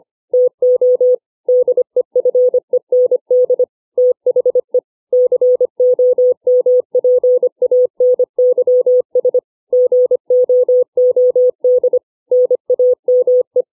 Morse code, heard as a keyed tone, about 25 wpm, 500 Hz. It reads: TO DEFEND THE COMPANYS GOOD NAME